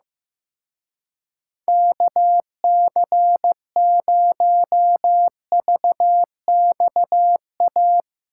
Morse code, heard as a keyed tone, about 16 wpm, 700 Hz. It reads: KC0VXA
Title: Speed Racing - USA 2x3 Call Signs 15wpm